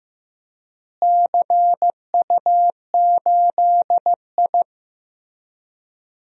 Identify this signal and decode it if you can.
Morse code, heard as a keyed tone, about 15 wpm, 700 Hz. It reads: CU8I
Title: DXCC-100 Call Signs - 4 Characters - Repeated 3x - 15wpm